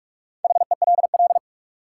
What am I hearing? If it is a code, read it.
Morse code, heard as a keyed tone, about 44 wpm, 700 Hz. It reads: HELL
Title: Sets of 5 Words Encore - New Word Drill - 45wpm